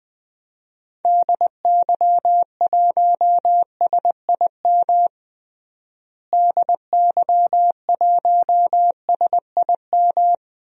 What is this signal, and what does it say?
Morse code, heard as a keyed tone, about 20 wpm, 700 Hz. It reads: DY1SIM DY1SIM
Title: DXCC-100 Call Signs - 6 Characters - Repeated 3x - 20wpm